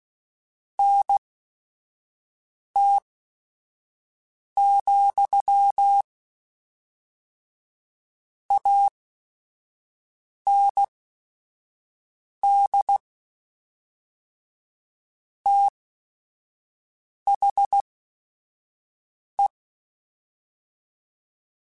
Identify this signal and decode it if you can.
Morse code, heard as a keyed tone, about 16 wpm, spaced as for 5 wpm, 800 Hz. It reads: NT, AND THE
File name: Technical Subjects / Technical Subjects-mp3s / XL @5